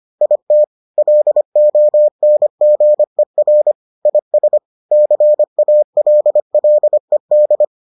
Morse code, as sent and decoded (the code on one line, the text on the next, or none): .. - / .-.. --- -. --. .-. / .. ... / -.-. .- .-.. .-.. -..
IT LONGER IS CALLED